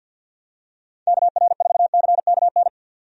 Morse code, heard as a keyed tone, about 51 wpm, 700 Hz. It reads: KR4CKN
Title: USA Call Signs - 6 Characters - 50wpm